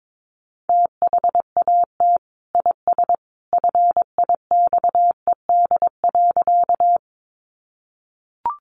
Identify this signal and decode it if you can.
Morse code, heard as a keyed tone, about 22 wpm, 700 Hz. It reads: THAT IS FIXED.
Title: Sentences from Top 1100 Words 22wpm